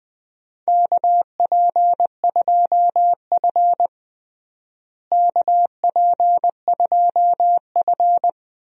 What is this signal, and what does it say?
Morse code, heard as a keyed tone, about 21 wpm, 700 Hz. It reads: KP2F KP2F
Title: DXCC-100 Call Signs - 4 Characters - Repeated 2x - 20wpm